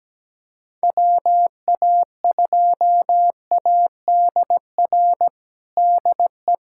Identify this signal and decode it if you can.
Morse code, heard as a keyed tone, about 17 wpm, 700 Hz. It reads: WA2ADR DE
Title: USA Callsign DE Callsign - 6 Characters - 17wpm